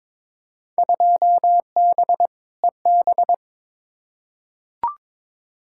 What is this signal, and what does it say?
Morse code, heard as a keyed tone, about 22 wpm, 700 Hz. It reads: 2B EB
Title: Field Day Exchanges - 22wpm